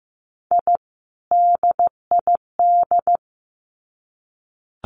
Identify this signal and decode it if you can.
Morse code, heard as a keyed tone, about 15 wpm, 700 Hz. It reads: I DID
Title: Sets of 2 Words 15wpm